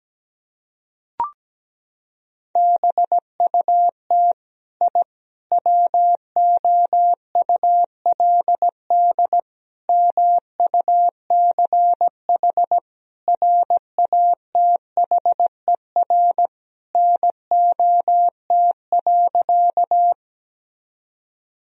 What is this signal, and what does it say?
Morse code, heard as a keyed tone, about 17 wpm, 700 Hz. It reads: BUT I WOULD MUCH RATHER NOT.
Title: Sentences from Top 1000 Words - Review Part 4 - 17wpm